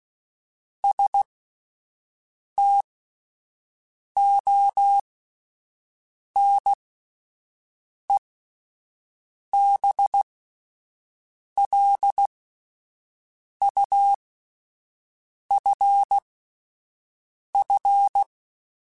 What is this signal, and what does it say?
Morse code, heard as a keyed tone, about 16 wpm, spaced as for 5 wpm, 800 Hz. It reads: STONE BLUFF